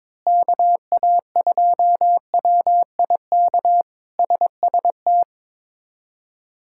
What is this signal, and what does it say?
Morse code, heard as a keyed tone, about 22 wpm, 700 Hz. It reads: KA2WIK SST
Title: K1USN SST QSO Practice - 22wpm